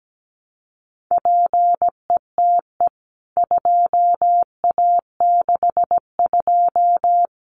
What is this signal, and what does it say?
Morse code, heard as a keyed tone, about 17 wpm, 700 Hz. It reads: PETE 2A62
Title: CWT Exchange 17wpm